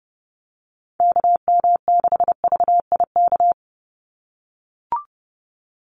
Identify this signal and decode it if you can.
Morse code, heard as a keyed tone, about 30 wpm, 700 Hz. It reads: KM6VIK
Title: USA Call Signs 30wpm